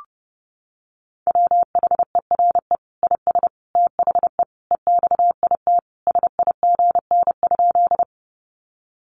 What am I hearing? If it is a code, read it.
Morse code, heard as a keyed tone, about 30 wpm, 700 Hz. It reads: WHERE IS THE EXIT SIGN?
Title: Sentences from Top 1100 Words 30wpm